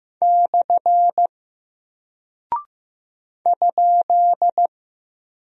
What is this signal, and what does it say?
Morse code, heard as a keyed tone, about 15 wpm, 700 Hz. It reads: / ?
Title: Single Character (Letter:Number:Punctuation:Prosign) Alt 15wpm